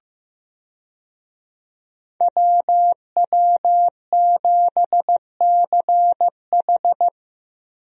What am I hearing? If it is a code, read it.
Morse code, heard as a keyed tone, about 15 wpm, 700 Hz. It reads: WW7CH
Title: USA Call Signs - 5 Characters - Repeated 2x - 15wpm